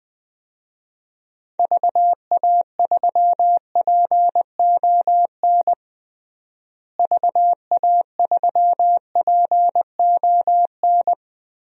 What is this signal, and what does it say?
Morse code, heard as a keyed tone, about 20 wpm, 700 Hz. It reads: VA3PON VA3PON
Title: DXCC-100 Call Signs - 6 Characters - Repeated 3x - 20wpm